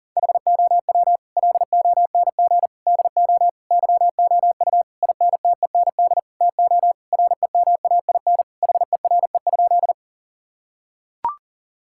Morse code, heard as a keyed tone, about 37 wpm, 700 Hz. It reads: HOW LONG DO YOU INTEND TO REMAIN HERE?
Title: Sentences from Top 1600 Words 40wpm